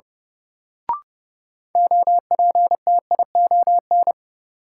Morse code, heard as a keyed tone, about 30 wpm, 700 Hz. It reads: OPTION
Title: Sentences from Top 1300 Words - New Word Drill - 30wpm